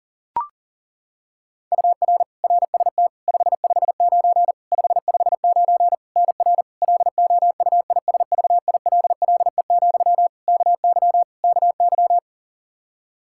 Morse code, heard as a keyed tone, about 40 wpm, 700 Hz. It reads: UR RST 559 559 NR LOUISVILLE, KY KY
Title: Practice QSO - Exchange 1 - Beginning - 40wpm